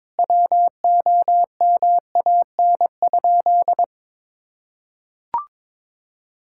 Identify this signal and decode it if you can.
Morse code, heard as a keyed tone, about 22 wpm, 700 Hz. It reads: WOMAN?
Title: Sentences from Top 1600 Words 22wpm